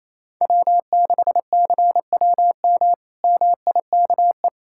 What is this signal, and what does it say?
Morse code, heard as a keyed tone, about 28 wpm, 700 Hz. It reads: W6CWM MIKE
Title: K1USN SST QSO Practice - 28wpm